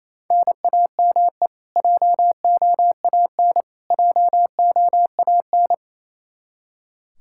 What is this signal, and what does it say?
Morse code, heard as a keyed tone, about 27 wpm, 700 Hz. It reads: NAME JOAN JOAN
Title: Practice QSO - Exchange 1 - Beginning - 28wpm